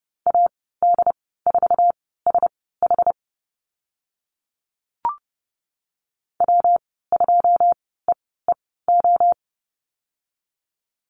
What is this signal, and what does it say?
Morse code, heard as a keyed tone, about 30 wpm, spaced as for 17 wpm, 700 Hz. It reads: AD4SH W2EEO
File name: Lesson 19g - Letter w - Call Signs 30wpm